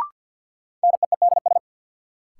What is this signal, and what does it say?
Morse code, heard as a keyed tone, about 53 wpm, 700 Hz. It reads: NEEDS
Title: Sentences from Top 1000 Words - Review Part 3 - New Word Drill - 50wpm